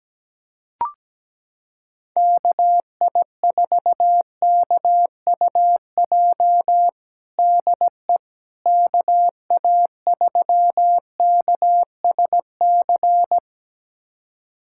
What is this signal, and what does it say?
Morse code, heard as a keyed tone, about 17 wpm, 700 Hz. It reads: KI4KUJ DE KA3KSC
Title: USA Callsign DE Callsign 17wpm